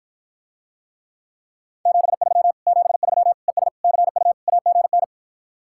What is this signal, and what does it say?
Morse code, heard as a keyed tone, about 54 wpm, 700 Hz. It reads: 73 73 ES CU AGN